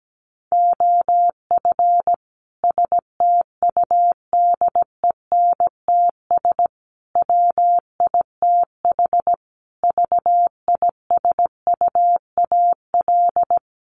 Morse code, heard as a keyed tone, about 17 wpm, 700 Hz. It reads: OF STUDENTS WITH VISUAL
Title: Sets of 5 Words Encore 17wpm